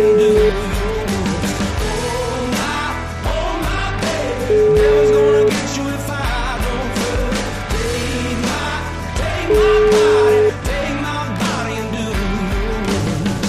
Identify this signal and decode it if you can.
Ringback tone (Europe, 425 Hz), with noise.